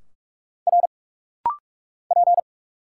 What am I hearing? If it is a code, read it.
Morse code, heard as a keyed tone, about 46 wpm, 700 Hz. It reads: R P
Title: Single Letters - Rapid-Fire 45wpm